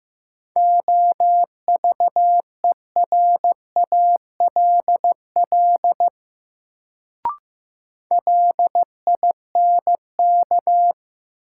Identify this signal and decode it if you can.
Morse code, heard as a keyed tone, about 15 wpm, 700 Hz. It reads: OVERALL LINK
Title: Sentences from Top 1000 Words - Review Part 7 - New Word Drill - 15wpm